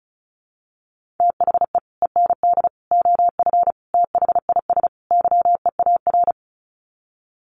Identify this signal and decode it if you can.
Morse code, heard as a keyed tone, about 34 wpm, 700 Hz. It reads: THE END OF THIS YEAR
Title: Sets of 5 Words Encore 35wpm